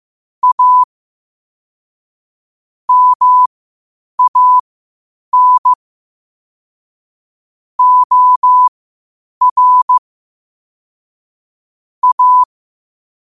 Morse code, heard as a keyed tone, about 15 wpm, spaced as for 8 wpm, 1000 Hz. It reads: A MAN OR A
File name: qotdfiles / 8wpm